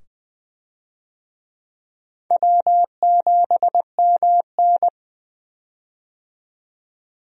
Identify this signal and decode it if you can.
Morse code, heard as a keyed tone, about 20 wpm, 700 Hz. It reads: W7MN
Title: USA Call Signs - 4 Characters - 20wpm